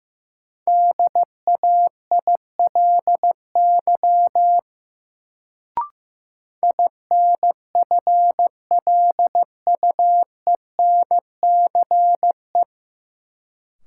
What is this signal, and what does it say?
Morse code, heard as a keyed tone, about 15 wpm, 700 Hz. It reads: DAILY INFLUENCE
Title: Sentences from Top 1100 Words - New Word Drill Alt - 15wpm